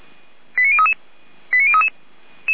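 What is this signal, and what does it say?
Five-tone selective call (ZVEI), heard as quick runs of five tones, tones of about 70 ms. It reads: 89033 89033
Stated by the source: ZVEI five-tone selective call 89042 received on an SDR.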